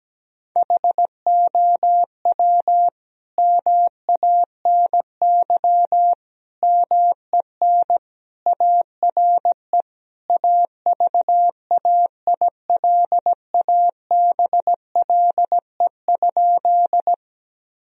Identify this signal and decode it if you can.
Morse code, heard as a keyed tone, about 17 wpm, 700 Hz. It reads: HOW MANY MEN ARE AVAILABLE?
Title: Sentences from Top 1000 Words - Review Part 3 - 17wpm